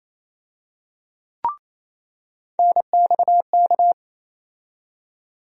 Morse code, heard as a keyed tone, about 28 wpm, 700 Hz. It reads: NXK
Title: USA Call Sign Postfixes - 3 Letters 28wpm